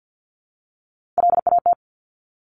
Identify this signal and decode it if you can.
Morse code, heard as a keyed tone, about 50 wpm, 700 Hz. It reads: LAT
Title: Top Three Letter Combinations in Words 50wpm